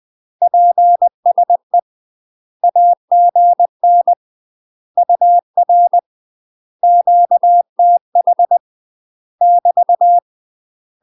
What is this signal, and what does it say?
Morse code, heard as a keyed tone, about 20 wpm, 700 Hz. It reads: PSE AGN UR QTH <BT>
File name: UK MASTER QSOs TRAINING / 20WPM / UK MASTER QSOs 0019